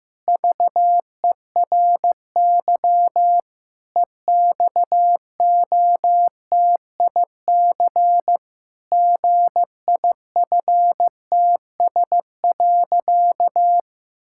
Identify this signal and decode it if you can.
Morse code, heard as a keyed tone, about 16 wpm, 700 Hz. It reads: VERY EXOTIC GIFTS.
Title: Pangram Challenge v2 - 15wpm